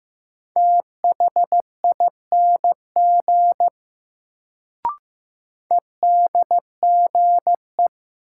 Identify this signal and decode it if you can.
Morse code, heard as a keyed tone, about 15 wpm, 700 Hz. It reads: THING EDGE
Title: Sets of 3 Words - New Word Drill - 15wpm